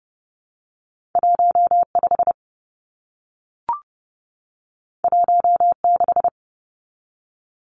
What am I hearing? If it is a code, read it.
Morse code, heard as a keyed tone, about 30 wpm, 700 Hz. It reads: 15 16
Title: Numbers - 2-Digit 30wpm